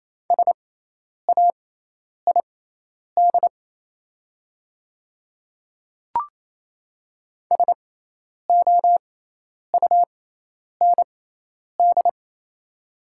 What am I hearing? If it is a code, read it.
Morse code, heard as a keyed tone, about 27 wpm, spaced as for 10 wpm, 700 Hz. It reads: SAID SOUND
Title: 6x Character Spacing - Top 100 Words - 28wpm